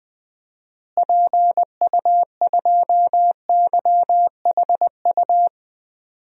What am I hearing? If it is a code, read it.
Morse code, heard as a keyed tone, about 20 wpm, 700 Hz. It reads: PU2YHU
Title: DXCC-100 Call Signs - 6 Characters - Repeated 2x - 20wpm